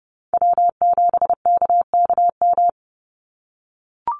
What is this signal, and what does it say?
Morse code, heard as a keyed tone, about 30 wpm, 700 Hz. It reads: W7KKM
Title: NA SOTA Chaser Callsigns 30wpm